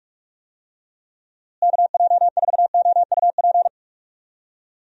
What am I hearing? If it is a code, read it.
Morse code, heard as a keyed tone, about 44 wpm, 700 Hz. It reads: KJ4OUP
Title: USA Call Signs - 6 Characters - Repeated 3x - 45wpm